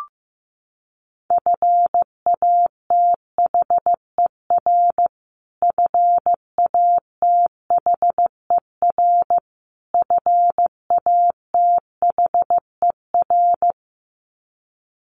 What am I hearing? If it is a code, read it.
Morse code, heard as a keyed tone, about 15 wpm, 700 Hz. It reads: FATHER FATHER FATHER
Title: Top 500 Words Repeated 15wpm